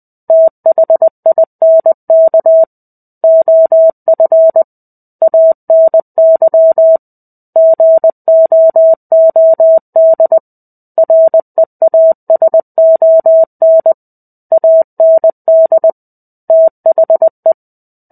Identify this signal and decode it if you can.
Morse code, heard as a keyed tone, about 20 wpm, 650 Hz. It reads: THINK OF ANY GOOD REASON AND THE
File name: 05-Chapter-05 20 WPM